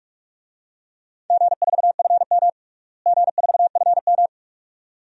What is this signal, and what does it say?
Morse code, heard as a keyed tone, about 48 wpm, 700 Hz. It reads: G4FM G4FM
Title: DXCC-100 Call Signs - 4 Characters - Repeated 2x - 45wpm